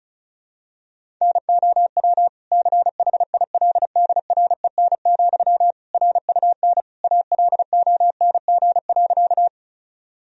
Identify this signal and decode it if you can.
Morse code, heard as a keyed tone, about 36 wpm, 700 Hz. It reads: NOW CHILDREN, RUN ALONG.
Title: Sentences from Top 1000 Words - Review Part 1 - 35wpm